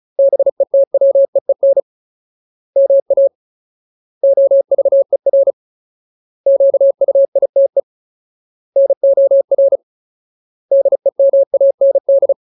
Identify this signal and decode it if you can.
Morse code, heard as a keyed tone, about 35 wpm, 550 Hz. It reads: BETWEEN MA OVER QUITE NOR DEMAND
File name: Ultimate Word List / 35WPM Ultimate